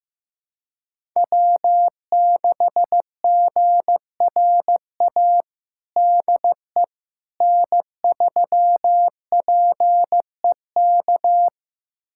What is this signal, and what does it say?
Morse code, heard as a keyed tone, about 15 wpm, 700 Hz. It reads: W6GRA DE N3PEK
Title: USA Callsign DE Callsign 15wpm